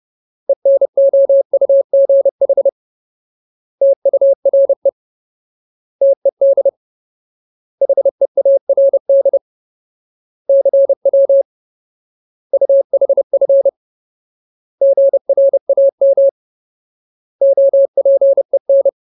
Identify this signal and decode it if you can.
Morse code, heard as a keyed tone, about 30 wpm, 550 Hz. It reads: ENOUGH TURE TED HEARD CW UHF GRAM OPEN
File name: Ultimate Word List / 30WPM Ultimate